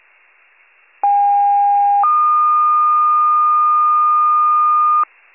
Two-tone page, heard as a one-second tone, then a three-second tone, about 810 then 1220 Hz. It is heard from 1.0 to 5.0 s.